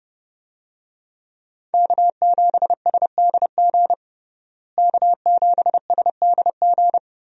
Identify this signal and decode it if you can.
Morse code, heard as a keyed tone, about 32 wpm, 700 Hz. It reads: K7SDG K7SDG
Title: USA Call Signs - 5 Characters - Repeated 3x - 30wpm